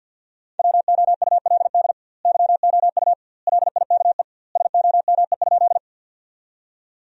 Morse code, heard as a keyed tone, about 50 wpm, 700 Hz. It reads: WOULD YOU LIKE SOME?
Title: 2x Word-Spacing - Sentences from Top 100 Words - 50wpm